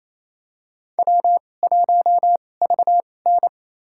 Morse code, heard as a keyed tone, about 28 wpm, spaced as for 20 wpm, 700 Hz. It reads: W1VN
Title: USA Call Signs - 4 Characters - 2x Character Spacing - 28wpm